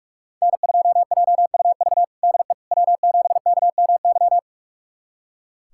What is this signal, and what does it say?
Morse code, heard as a keyed tone, about 43 wpm, 700 Hz. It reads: N2JUV DE W7KMY
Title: USA Callsign DE Callsign - 5 Characters - 45wpm